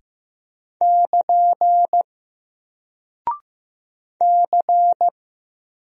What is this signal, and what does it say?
Morse code, heard as a keyed tone, about 15 wpm, 700 Hz. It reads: <KN> C
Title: Single Character (Letter:Number:Punctuation:Prosign) Alt 15wpm